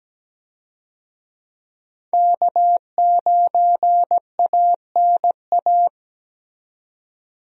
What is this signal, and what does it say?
Morse code, heard as a keyed tone, about 17 wpm, 700 Hz. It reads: K9ANA
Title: USA Call Signs - 5 Characters - 17wpm